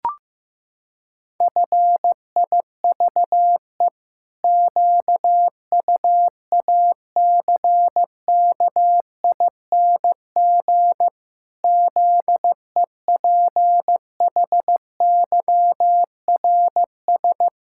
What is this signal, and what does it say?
Morse code, heard as a keyed tone, about 15 wpm, 700 Hz. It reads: FIVE QUACKING ZEPHYRS
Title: Pangram Challenge v2 - 15wpm